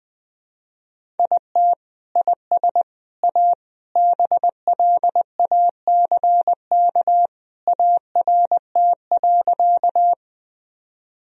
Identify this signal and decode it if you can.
Morse code, heard as a keyed tone, about 21 wpm, 700 Hz. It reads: IT IS A BLACK ART.
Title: Sentences from Top 1000 Words - Review Part 3 - 20wpm